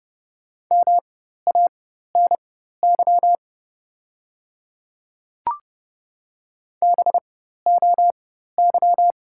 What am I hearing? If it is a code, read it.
Morse code, heard as a keyed tone, about 30 wpm, spaced as for 14 wpm, 700 Hz. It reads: MANY BOY
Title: 4x Character Spacing - Top 300 Words - 30wpm